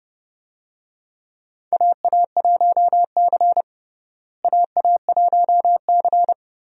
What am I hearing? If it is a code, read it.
Morse code, heard as a keyed tone, about 31 wpm, 700 Hz. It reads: AA1C AA1C
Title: USA Call Signs - 4 Characters - Repeated 2x - 30wpm